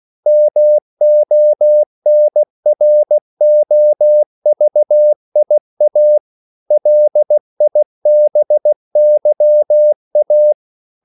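Morse code, heard as a keyed tone, about 16 wpm, 600 Hz. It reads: MONROVIA LIBYA
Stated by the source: CW practice of national capitals at 16 words per minute, from ON6ZQ.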